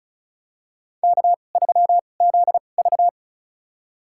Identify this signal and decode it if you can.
Morse code, heard as a keyed tone, about 36 wpm, spaced as for 25 wpm, 700 Hz. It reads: K3ZV
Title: USA Call Signs - 4 Characters - 2x Character Spacing - 35wpm